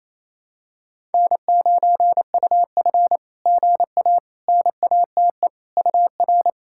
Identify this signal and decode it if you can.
Morse code, heard as a keyed tone, about 29 wpm, 700 Hz. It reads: N9UF GA NATE UR
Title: SOTA QSO Practice 28wpm